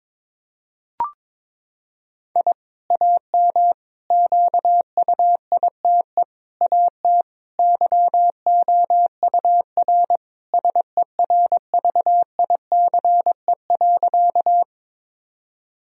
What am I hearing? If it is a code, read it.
Morse code, heard as a keyed tone, about 22 wpm, 700 Hz. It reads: I AM QUITE AT YOUR SERVICE.
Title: Sentences from Top 1000 Words - Review Part 4 - 22wpm